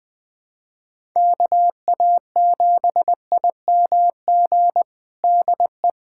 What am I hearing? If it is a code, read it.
Morse code, heard as a keyed tone, about 20 wpm, 700 Hz. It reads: KA7IMG DE
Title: USA Callsign DE Callsign - 6 Characters - 20wpm